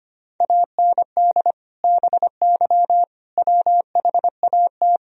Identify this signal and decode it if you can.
Morse code, heard as a keyed tone, about 25 wpm, 700 Hz. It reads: AND BY WHAT
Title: Sentences from Top 1000 Words - Review Part 3 - 25wpm